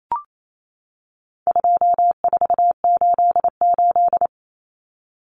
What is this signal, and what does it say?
Morse code, heard as a keyed tone, about 28 wpm, 700 Hz. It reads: 2488
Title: Numbers - 4-Digit 28wpm